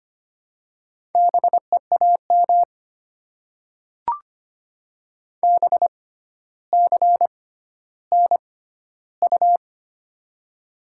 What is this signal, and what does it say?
Morse code, heard as a keyed tone, about 25 wpm, spaced as for 13 wpm, 690 Hz. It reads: BEAM BCNU
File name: Top 90 QSO Elements - 6x Character Spacing - 25wpm